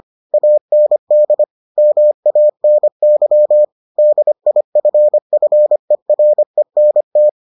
Morse code, heard as a keyed tone, about 25 wpm, 600 Hz. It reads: AND MANY DIFFERENT